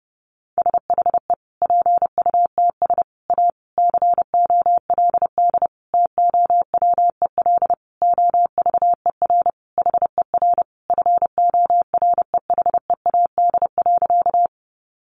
Morse code, heard as a keyed tone, about 30 wpm, 700 Hz. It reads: SHE PUTS A COLD TOWEL OVER HER FOREHEAD.
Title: Easy Phrases 30wpm